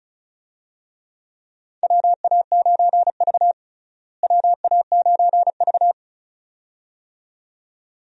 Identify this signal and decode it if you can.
Morse code, heard as a keyed tone, about 36 wpm, 700 Hz. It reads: WA9V WA9V
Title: USA Call Signs - 4 Characters - Repeated 2x - 35wpm